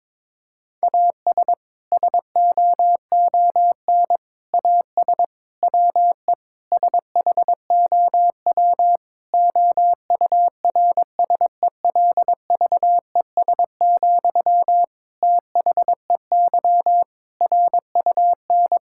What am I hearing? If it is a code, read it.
Morse code, heard as a keyed tone, about 22 wpm, 700 Hz. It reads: AS SOON AS WE SHOW OURSELVES, THEY RUN
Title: Sentences from Top 1600 Words 22wpm